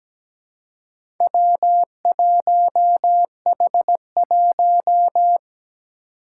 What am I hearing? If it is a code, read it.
Morse code, heard as a keyed tone, about 17 wpm, 700 Hz. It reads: W1H1
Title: Four Challenging Letter-Number Combinations 17wpm